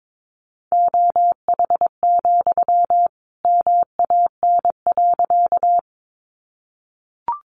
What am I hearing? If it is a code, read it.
Morse code, heard as a keyed tone, about 22 wpm, 700 Hz. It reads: OH, MAN.
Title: Sentences from Top 500 Words 22wpm